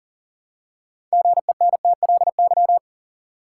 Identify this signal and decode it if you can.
Morse code, heard as a keyed tone, about 40 wpm, 700 Hz. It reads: GENTLY
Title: Six Letter Words - 40wpm